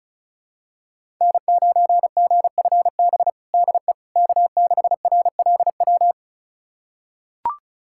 Morse code, heard as a keyed tone, about 34 wpm, 700 Hz. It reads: N9GFB DE K6RLW
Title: USA Callsign DE Callsign - 5 Characters - 35wpm